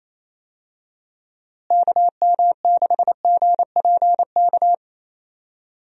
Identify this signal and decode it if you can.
Morse code, heard as a keyed tone, about 28 wpm, 700 Hz. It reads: KM6GPK
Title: USA Call Signs - 6 Characters - Repeated 2x - 28wpm